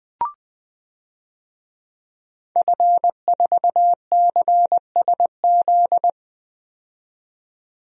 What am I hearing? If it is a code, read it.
Morse code, heard as a keyed tone, about 20 wpm, 700 Hz. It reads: F4CSZ
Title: DXCC-100 Call Signs - 5 Characters - 20wpm